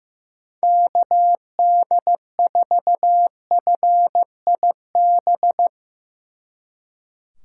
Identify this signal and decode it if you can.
Morse code, heard as a keyed tone, about 16 wpm, 700 Hz. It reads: KD4FIB
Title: Speed Racing - USA 2x3 Call Signs 15wpm